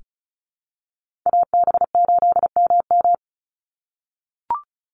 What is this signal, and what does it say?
Morse code, heard as a keyed tone, about 34 wpm, 700 Hz. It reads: AB8MM
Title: USA Call Signs 35wpm